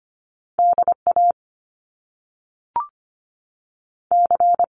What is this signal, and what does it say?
Morse code, heard as a keyed tone, about 25 wpm, 700 Hz. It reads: DA C
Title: Top Two Letter Combinations in Words 25wpm